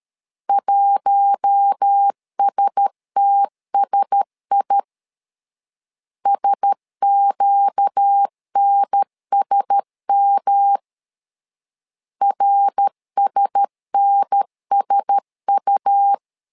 Morse code, heard as a keyed tone, about 13 wpm, 800 Hz. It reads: 1STSI SQNSM RSNSU
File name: LEZIONI / 4 Q1S 12wpm 1802